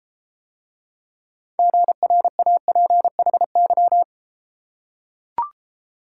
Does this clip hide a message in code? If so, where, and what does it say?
Yes: Morse code from 1.6 to 4.0 s, heard as a keyed tone, about 33 wpm, 700 Hz. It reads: GRAPHY